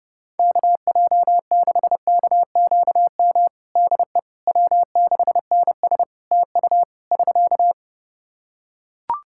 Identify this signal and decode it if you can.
Morse code, heard as a keyed tone, about 30 wpm, 700 Hz. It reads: KJ6KQM DE W6NS TU <SK>
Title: Practice QSO - Exchange 4 - Ending - 30wpm